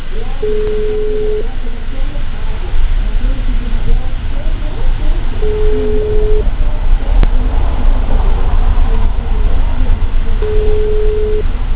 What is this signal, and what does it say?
Ringback tone (Europe, 425 Hz), with speech.
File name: out- -2070- - - .96